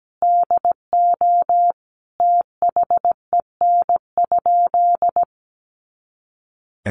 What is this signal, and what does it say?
Morse code, heard as a keyed tone, about 17 wpm, 700 Hz. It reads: DO THEN?
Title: Sentences from Top 100 Words 17wpm